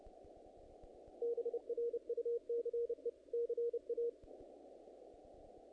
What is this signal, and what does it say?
Morse code, heard as a keyed tone, about 30 wpm, 470 Hz. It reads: BRUCE CA
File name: contests / naqp-cw-k5tr-op-k5pi- -1900